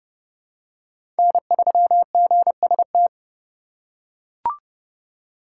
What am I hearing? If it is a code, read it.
Morse code, heard as a keyed tone, about 31 wpm, 700 Hz. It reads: N3GST
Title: USA Call Signs - 5 Characters - 2x Character Spacing - 30wpm